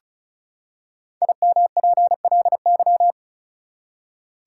Morse code, heard as a keyed tone, about 36 wpm, 700 Hz. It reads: IMPLY
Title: Five Letter Words - 35wpm